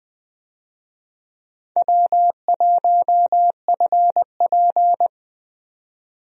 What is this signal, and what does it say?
Morse code, heard as a keyed tone, about 20 wpm, 700 Hz. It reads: W1FP
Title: USA Call Signs - 4 Characters - 20wpm